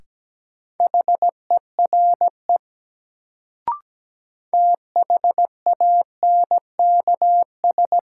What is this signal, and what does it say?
Morse code, heard as a keyed tone, about 17 wpm, 700 Hz. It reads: HERE THANKS
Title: Sets of 5 Words - New Word Drill - 17wpm